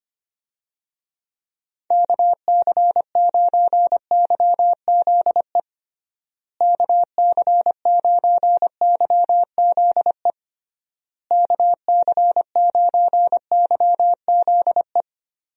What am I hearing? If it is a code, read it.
Morse code, heard as a keyed tone, about 25 wpm, 700 Hz. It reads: KC9YZE KC9YZE KC9YZE
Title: USA Call Signs - 6 Characters - Repeated 3x - 25wpm